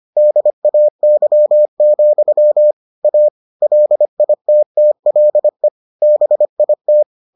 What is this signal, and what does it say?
Morse code, heard as a keyed tone, about 25 wpm, 600 Hz. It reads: DAY, A LITTLE BIT